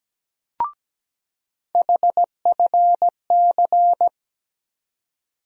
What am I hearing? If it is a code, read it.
Morse code, heard as a keyed tone, about 17 wpm, 700 Hz. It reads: HFC
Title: USA Call Sign Postfixes - 3 Letters 17wpm